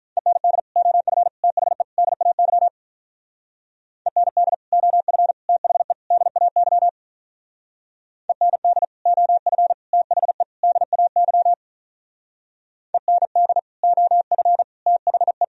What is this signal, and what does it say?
Morse code, heard as a keyed tone, about 43 wpm, 700 Hz. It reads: END OF THE DAY END OF THE DAY END OF THE DAY END OF THE
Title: Speed-Racing - Sets of 4 Words 35wpm